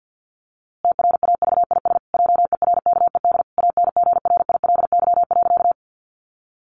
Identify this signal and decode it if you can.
Morse code, heard as a keyed tone, about 50 wpm, 700 Hz. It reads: TRAVIS JERKED ANGRILY.
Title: A Sound Of Thunder 50wpm